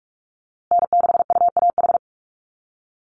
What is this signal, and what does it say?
Morse code, heard as a keyed tone, about 44 wpm, 700 Hz. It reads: N6UAH
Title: USA Call Signs Repeated 2x 45wpm